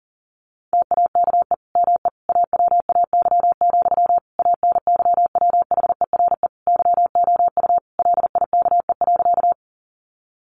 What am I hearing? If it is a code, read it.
Morse code, heard as a keyed tone, about 40 wpm, 700 Hz. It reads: TAKE ME AWAY, ANYWHERE YOU LIKE.
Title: Sentences from Top 1400 Words 40wpm